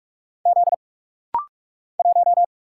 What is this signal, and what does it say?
Morse code, heard as a keyed tone, about 48 wpm, 700 Hz. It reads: Z 1
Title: Single Letter-Number - Rapid-Fire 45wpm